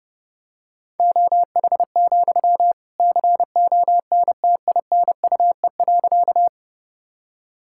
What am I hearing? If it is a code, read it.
Morse code, heard as a keyed tone, about 30 wpm, 700 Hz. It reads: OH, CONTINUE.
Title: Sentences from Top 1000 Words - Review Part 6 - 30wpm